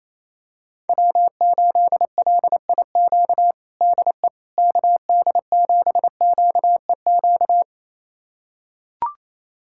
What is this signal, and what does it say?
Morse code, heard as a keyed tone, about 28 wpm, 700 Hz. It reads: W8LIQ DE KD7QEQ
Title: USA Callsign DE Callsign 28wpm